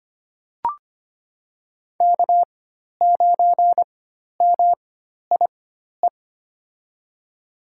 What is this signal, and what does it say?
Morse code, heard as a keyed tone, about 25 wpm, spaced as for 12 wpm, 700 Hz. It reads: K9MIE
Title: USA Call Signs - 4x Character Spacing - 25wpm